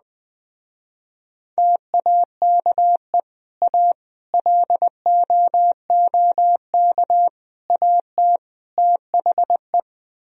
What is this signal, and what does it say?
Morse code, heard as a keyed tone, about 20 wpm, 700 Hz. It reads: TAKE A LOOK AT THE